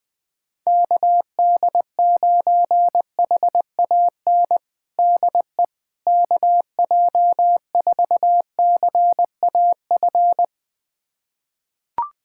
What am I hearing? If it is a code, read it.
Morse code, heard as a keyed tone, about 20 wpm, 700 Hz. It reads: KD9HAN DE KJ4CAF
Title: USA Callsign DE Callsign 20wpm